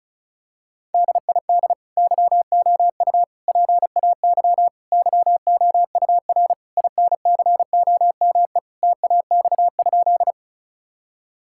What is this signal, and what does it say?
Morse code, heard as a keyed tone, about 36 wpm, 700 Hz. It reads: DID YOU PAY YOUR INCOME TAX?
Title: Sentences from Top 1000 Words - Review Part 5 - 35wpm